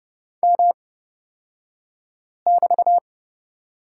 Morse code, heard as a keyed tone, about 30 wpm, spaced as for 5 wpm, 700 Hz. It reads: M<BT>
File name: Lesson 38d - Letter q - Two Characters 30wpm